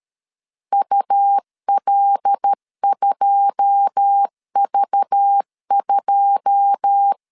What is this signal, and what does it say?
Morse code, heard as a keyed tone, about 13 wpm, 800 Hz. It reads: UL2V2
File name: LEZIONI / 9 2 12wpm 2301